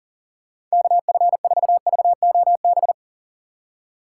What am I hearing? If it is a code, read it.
Morse code, heard as a keyed tone, about 40 wpm, 700 Hz. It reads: KF4VOB
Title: USA Call Signs - 6 Characters - 6x Character Spacing - 40wpm